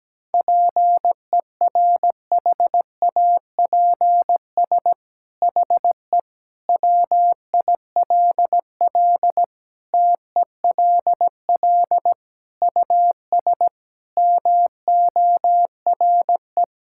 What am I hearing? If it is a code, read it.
Morse code, heard as a keyed tone, about 17 wpm, 700 Hz. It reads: PERHAPS HE WILL TELL US MORE
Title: Sentences from Top 1600 Words 17wpm